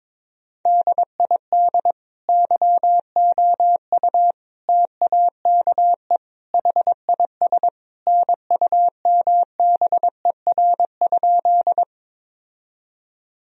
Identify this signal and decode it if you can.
Morse code, heard as a keyed tone, about 22 wpm, 700 Hz. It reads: DID YOU TAKE HIS NUMBER?
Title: Sentences from Top 1000 Words - Review Part 1 - 22wpm